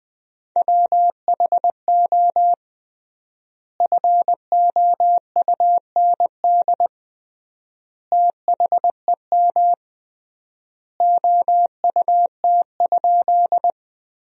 Morse code, heard as a keyed tone, about 20 wpm, 700 Hz. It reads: WHO FOUND THEM OUT?
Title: 3x Word-Spacing - Sentences from Top 300 Words - 20wpm